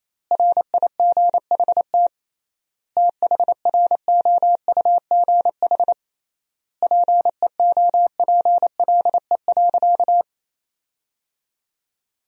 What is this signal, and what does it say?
Morse code, heard as a keyed tone, about 27 wpm, 700 Hz. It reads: RIGHT THROUGH PEOPLE.
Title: 3x Word-Spacing - Sentences from Top 200 Words - 28wpm